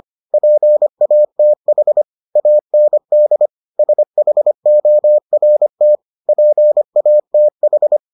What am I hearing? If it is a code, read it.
Morse code, heard as a keyed tone, about 25 wpm, 600 Hz. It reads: PATH AND SHORT PATH